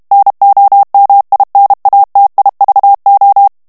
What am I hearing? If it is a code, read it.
Morse code, heard as a keyed tone, about 32 wpm, 800 Hz. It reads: NOMINATIVO
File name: ITALIEN texte clair 2 vitesse 30 mots minute